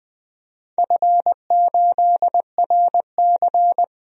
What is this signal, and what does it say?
Morse code, heard as a keyed tone, about 20 wpm, 700 Hz. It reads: F8RC
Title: DXCC-100 Call Signs - 4 Characters - Repeated 2x - 20wpm